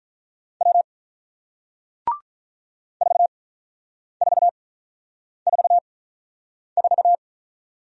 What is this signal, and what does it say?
Morse code, heard as a keyed tone, about 48 wpm, spaced as for 9 wpm, 700 Hz. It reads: W 4444